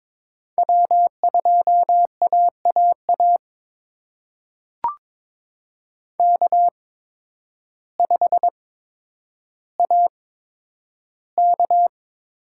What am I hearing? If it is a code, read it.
Morse code, heard as a keyed tone, about 22 wpm, spaced as for 10 wpm, 700 Hz. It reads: W2AAA K5AK
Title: USA Call Signs - 5 Characters - 8x Character Spacing - 22wpm